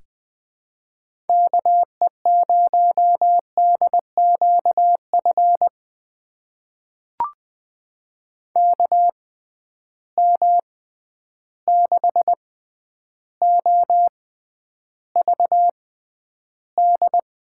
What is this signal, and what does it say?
Morse code, heard as a keyed tone, about 20 wpm, spaced as for 10 wpm, 700 Hz. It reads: KE0DQF KM6OVD
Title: USA Call Signs - 6 Characters - 6x Character Spacing - 20wpm